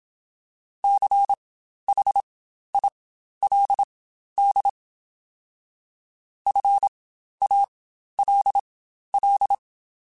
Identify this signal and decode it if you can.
Morse code, heard as a keyed tone, about 27 wpm, spaced as for 12 wpm, 800 Hz. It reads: CHILD FALL
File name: Quotes Spaced / Quotes Spaced-mp3s / ChildAsleep sp @25